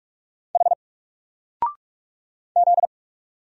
Morse code, heard as a keyed tone, about 46 wpm, 700 Hz. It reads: H Z
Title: Single Letter-Number Alt 45wpm